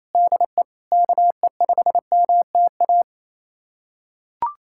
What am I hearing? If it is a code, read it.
Morse code, heard as a keyed tone, about 28 wpm, 700 Hz. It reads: DE KE5MTA
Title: USA Callsign DE Callsign 28wpm